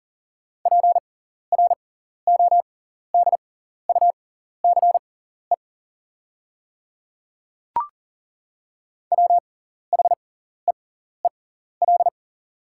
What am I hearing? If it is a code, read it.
Morse code, heard as a keyed tone, about 40 wpm, spaced as for 14 wpm, 690 Hz. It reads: PRODUCE WHEEL
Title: 6x Character Spacing - Top 500 Words - 40wpm